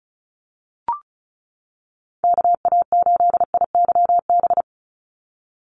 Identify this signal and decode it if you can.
Morse code, heard as a keyed tone, about 34 wpm, 700 Hz. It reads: KA8IYB
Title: USA Call Signs 35wpm